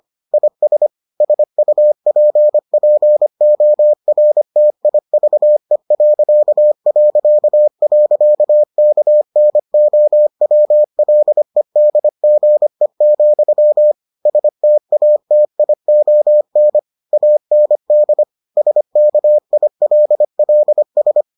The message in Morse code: IS SUPPORTIVE...KNOWLEDGE, STATION AND SKILLS

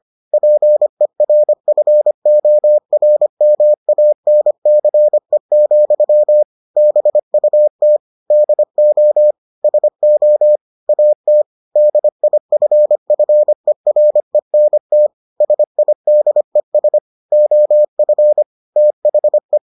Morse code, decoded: PERFORMANCE, BUT DO SO AT DIFFERENT SIDES OF THE

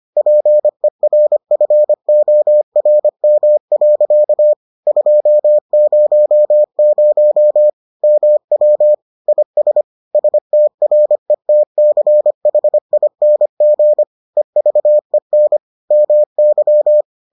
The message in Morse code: PERFORM. 200 MW IS STRETCHING EVEN MY

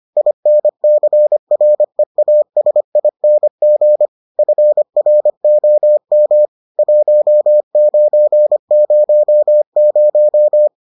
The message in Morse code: INCREASING FROM 1900

